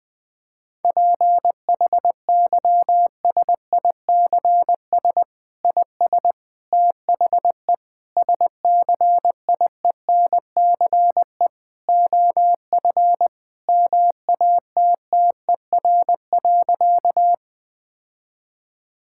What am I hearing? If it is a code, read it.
Morse code, heard as a keyed tone, about 20 wpm, 700 Hz. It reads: PHYSICS IS THE SCIENCE OF MATTER.
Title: Sentences from Top 1000 Words - Review Part 5 - 20wpm